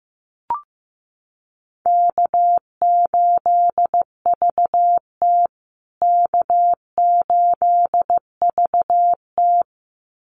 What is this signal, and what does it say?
Morse code, heard as a keyed tone, about 15 wpm, 700 Hz. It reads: K8VT K8VT
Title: USA Call Signs Repeated 2x 15wpm